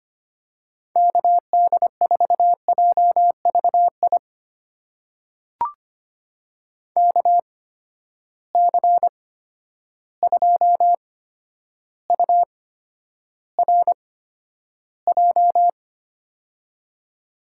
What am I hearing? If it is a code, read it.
Morse code, heard as a keyed tone, about 25 wpm, spaced as for 11 wpm, 700 Hz. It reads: KD4JVI KC2URJ
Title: USA Call Signs - 6 Characters - 8x Character Spacing - 25wpm